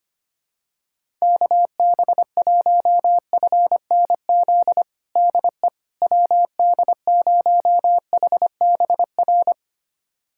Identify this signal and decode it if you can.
Morse code, heard as a keyed tone, about 25 wpm, 700 Hz. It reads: KB1FNZ DE WD0HBR
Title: USA Callsign DE Callsign - 6 Characters - 25wpm